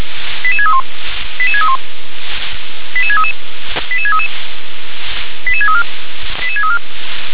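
Five-tone selective call (ZVEI), heard as quick runs of five tones, tones of about 70 ms. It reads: 88631 88631 88633 88633 88634 88634
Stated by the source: Five-tone ZVEI selective call received on SDR.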